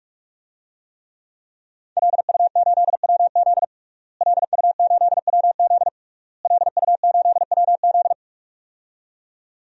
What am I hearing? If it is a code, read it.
Morse code, heard as a keyed tone, about 46 wpm, 700 Hz. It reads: LU8WZ LU8WZ LU8WZ
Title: DXCC-100 Call Signs - 5 Characters - Repeated 3x - 45wpm